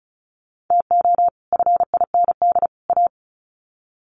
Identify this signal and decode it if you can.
Morse code, heard as a keyed tone, about 34 wpm, 700 Hz. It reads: TO FIND A
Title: Sets of 3 Words 35wpm